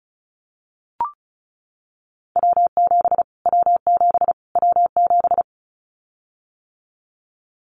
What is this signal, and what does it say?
Morse code, heard as a keyed tone, about 34 wpm, 700 Hz. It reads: W7 W7 W7
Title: Callsign Prefixes for Top 25 DXCC - Repeated 3x - 35wpm